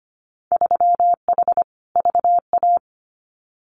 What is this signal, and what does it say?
Morse code, heard as a keyed tone, about 25 wpm, 700 Hz. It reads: 3H VA
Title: Winter Field Day Exchanges Encore 25wpm